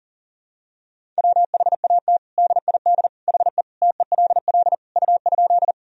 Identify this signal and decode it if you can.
Morse code, heard as a keyed tone, about 37 wpm, 700 Hz. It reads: WHAT DID HE TELL U?
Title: Sentences from Top 200 Words and Abbreviations 40wpm